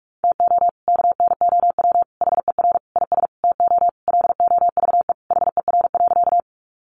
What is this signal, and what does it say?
Morse code, heard as a keyed tone, about 44 wpm, 700 Hz. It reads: TO KNOW HER IS TO LOVE HER.
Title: Sentences from Top 400 Words 45wpm